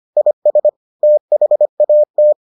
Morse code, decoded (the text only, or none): IS THAT